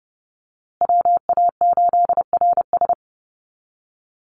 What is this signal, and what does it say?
Morse code, heard as a keyed tone, about 30 wpm, 700 Hz. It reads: WA8RS
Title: USA Call Signs Repeated 3x 30wpm